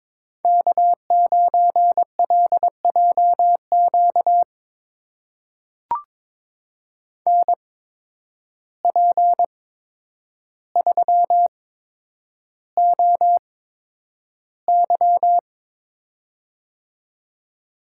Morse code, heard as a keyed tone, about 22 wpm, spaced as for 9 wpm, 700 Hz. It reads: K9LJQ NP3OY
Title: USA Call Signs - 5 Characters - 8x Character Spacing - 22wpm